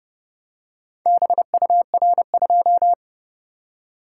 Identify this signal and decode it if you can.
Morse code, heard as a keyed tone, about 29 wpm, 700 Hz. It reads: BUR2